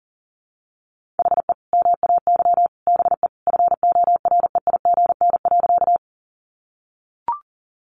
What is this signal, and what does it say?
Morse code, heard as a keyed tone, about 40 wpm, 700 Hz. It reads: HE MAY BE FOREIGN.
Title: Sentences from Top 1400 Words 40wpm